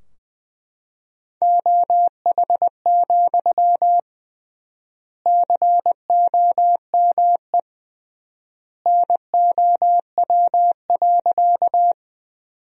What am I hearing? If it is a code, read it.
Morse code, heard as a keyed tone, about 21 wpm, 700 Hz. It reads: OH, COME NOW.
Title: 3x Word-Spacing - Sentences from Top 500 Words - 20wpm